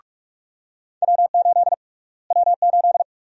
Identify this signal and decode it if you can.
Morse code, heard as a keyed tone, about 42 wpm, 700 Hz. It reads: W8 W8
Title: USA Call Sign Prefixes - Repeated 2x - 45wpm